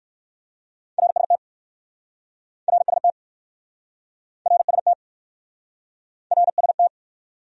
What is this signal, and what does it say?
Morse code, heard as a keyed tone, about 61 wpm, 700 Hz. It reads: RST RST RST RST